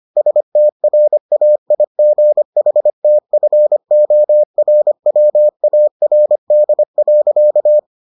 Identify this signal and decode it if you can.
Morse code, heard as a keyed tone, about 25 wpm, 600 Hz. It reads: STRAIGHTFORWARD.